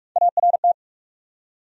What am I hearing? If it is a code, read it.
Morse code, heard as a keyed tone, about 46 wpm, 700 Hz. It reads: ART
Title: USA Common Names - Max 3 Letters - 45wpm